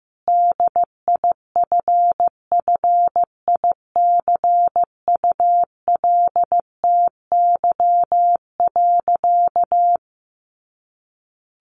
Morse code, heard as a keyed tone, about 15 wpm, 700 Hz. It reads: DIFFICULTY.
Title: Sentences from Top 1500 Words 15wpm